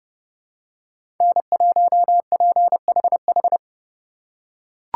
Morse code, heard as a keyed tone, about 28 wpm, 700 Hz. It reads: N1PHH